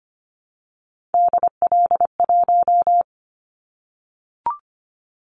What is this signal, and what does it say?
Morse code, heard as a keyed tone, about 25 wpm, 700 Hz. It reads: DL1
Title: Callsign Prefixes for Top 50 DXCC - 25wpm